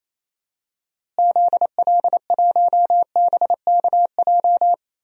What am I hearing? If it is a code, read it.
Morse code, heard as a keyed tone, about 28 wpm, 700 Hz. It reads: ZL1BKJ
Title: DXCC-100 Call Signs - 6 Characters - Repeated 2x - 28wpm